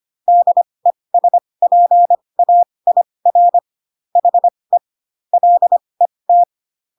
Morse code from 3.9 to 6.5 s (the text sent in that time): HE LET